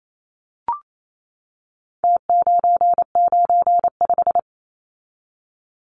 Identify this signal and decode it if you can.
Morse code, heard as a keyed tone, about 28 wpm, 700 Hz. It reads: T995
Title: Numbers - Contest Style - 4-Digit 28wpm